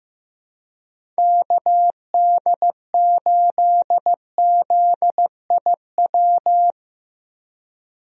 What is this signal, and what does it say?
Morse code, heard as a keyed tone, about 15 wpm, 700 Hz. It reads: KD8ZIW
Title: USA Call Signs - 6 Characters - Repeated 2x - 15wpm